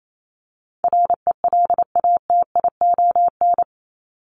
Morse code, heard as a keyed tone, about 28 wpm, 700 Hz. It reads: RELATION
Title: ES:S Suffix - Top 100 Words with Base Words 28wpm